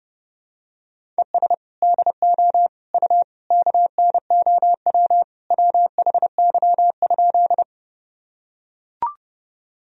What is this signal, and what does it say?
Morse code, heard as a keyed tone, about 30 wpm, 700 Hz. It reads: ES DO U KNOW WHY?
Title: Sentences from Top 200 Words and Abbreviations 30wpm